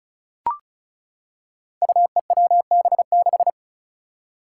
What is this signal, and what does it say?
Morse code, heard as a keyed tone, about 34 wpm, 700 Hz. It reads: UEWB6
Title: Five Challenging Letter-Number Combinations 35wpm